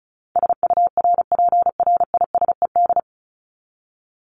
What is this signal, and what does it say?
Morse code, heard as a keyed tone, about 34 wpm, 700 Hz. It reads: SURPRISED
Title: Top 1501-1750 Words 35wpm